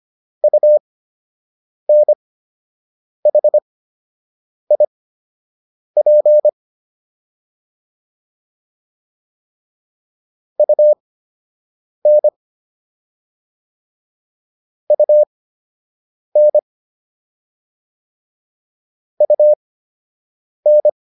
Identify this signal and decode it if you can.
Morse code, heard as a keyed tone, about 25 wpm, spaced as for 7 wpm, 600 Hz. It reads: UNHIP UN UN UN